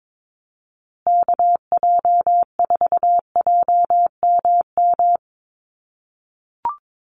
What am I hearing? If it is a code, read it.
Morse code, heard as a keyed tone, about 22 wpm, 700 Hz. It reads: KJ4JMM
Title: USA Call Signs Repeated 3x 22wpm